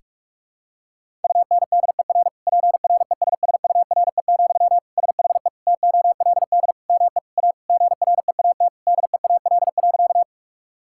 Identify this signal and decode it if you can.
Morse code, heard as a keyed tone, about 47 wpm, 700 Hz. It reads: UNDER PRESSURE, SHE TOLD ME A GREAT DEAL.
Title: Sentences from Top 1000 Words - Review Part 6 - 45wpm